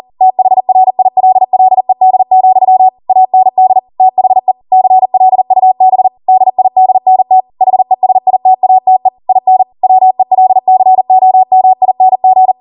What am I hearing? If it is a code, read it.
Morse code, heard as a keyed tone, about 40 wpm, 750 Hz. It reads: THRILLED, AND THE CLUB DIDNT HESITATE IN WELCOMING